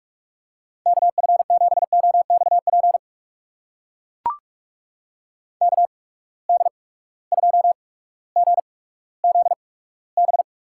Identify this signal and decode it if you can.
Morse code, heard as a keyed tone, about 44 wpm, spaced as for 19 wpm, 700 Hz. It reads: KF7OXP KD2GZB
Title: USA Call Signs - 8x Character Spacing - 45wpm